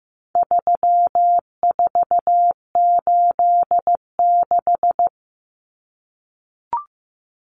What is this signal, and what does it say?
Morse code, heard as a keyed tone, about 15 wpm, 700 Hz. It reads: 3486
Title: Common Cut Numbers - 4-Digit 15wpm